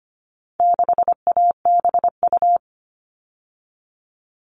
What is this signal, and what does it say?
Morse code, heard as a keyed tone, about 25 wpm, 700 Hz. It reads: 6ABU
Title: All Cut Numbers - 4-Digit 25wpm